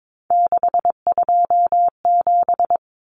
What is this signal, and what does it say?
Morse code, heard as a keyed tone, about 22 wpm, 700 Hz. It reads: 627
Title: Common Cut Numbers - 3-Digit 22wpm